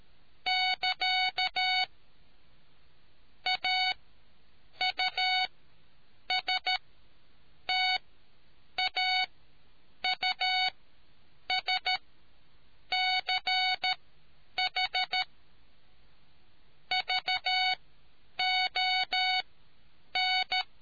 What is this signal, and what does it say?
Morse code, heard as a keyed tone, about 13 wpm, spaced as for 8 wpm, 740 Hz. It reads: <KA> AUSTAUSCH VON